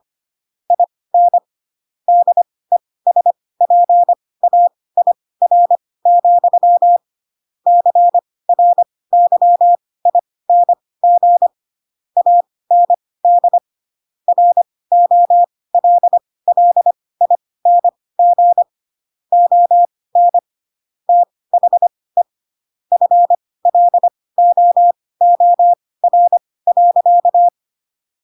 Morse code, heard as a keyed tone, about 25 wpm, spaced as for 18 wpm, 700 Hz. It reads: IN DESPAIR, CRYING AND ROLLING ON THE FLOOR.
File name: Pinocchio / Pinocchio ch7 12 25 ik1hge